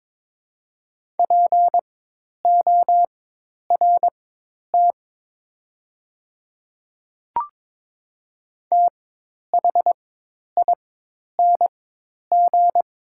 Morse code, heard as a keyed tone, about 22 wpm, spaced as for 10 wpm, 700 Hz. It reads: PORT THING
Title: 4x Character Spacing - Top 300 Words - 22wpm